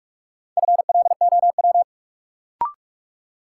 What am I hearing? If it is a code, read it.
Morse code, heard as a keyed tone, about 46 wpm, 700 Hz. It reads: FLOW
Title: Sentences from Top 1200 Words - New Word Drill Alt - 45wpm